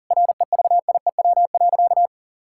Morse code, heard as a keyed tone, about 40 wpm, 700 Hz. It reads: REVIEW.
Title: Sentences from Top 1000 Words - Review Part 6 - 40wpm